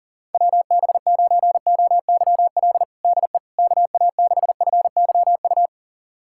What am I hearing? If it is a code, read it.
Morse code, heard as a keyed tone, about 40 wpm, 700 Hz. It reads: WB9OYL DE KA6FYU
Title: USA Callsign DE Callsign - 6 Characters - 40wpm